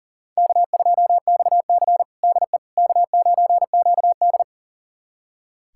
Morse code, heard as a keyed tone, about 40 wpm, 700 Hz. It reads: K2XC DE K9QD
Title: USA Callsign DE Callsign - 4 Characters - 40wpm